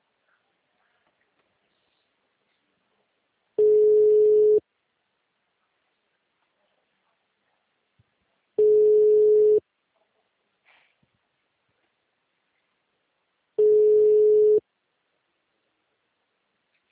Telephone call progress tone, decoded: ringback tone

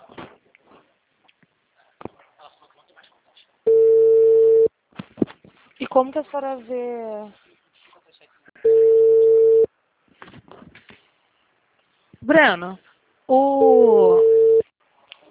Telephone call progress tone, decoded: ringback tone

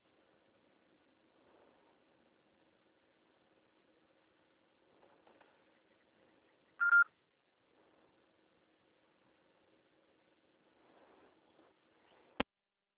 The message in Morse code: none

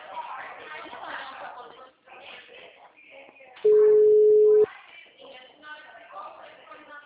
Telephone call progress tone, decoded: ringback tone

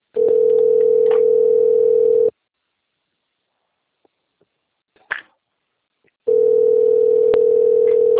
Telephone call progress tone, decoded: ringback tone

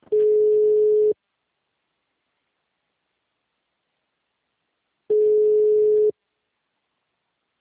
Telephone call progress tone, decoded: ringback tone

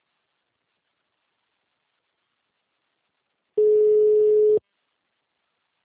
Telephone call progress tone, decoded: ringback tone

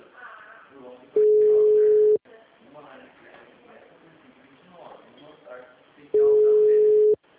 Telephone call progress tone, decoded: ringback tone

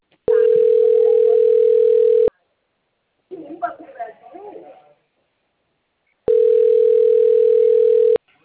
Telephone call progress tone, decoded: ringback tone

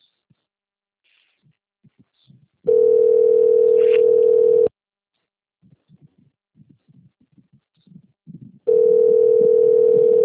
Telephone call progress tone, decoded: ringback tone

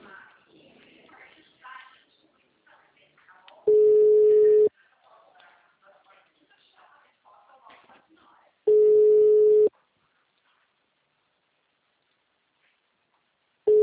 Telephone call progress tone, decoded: ringback tone

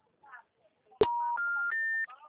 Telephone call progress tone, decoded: special information tone

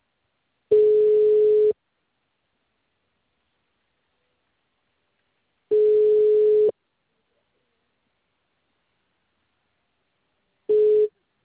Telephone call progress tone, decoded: ringback tone

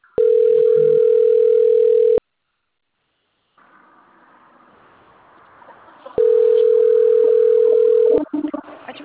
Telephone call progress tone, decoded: ringback tone

